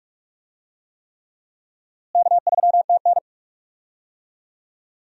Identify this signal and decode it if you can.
Morse code, heard as a keyed tone, about 46 wpm, 700 Hz. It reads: K3TN
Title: USA Call Signs - 4 Characters - Repeated 3x - 45wpm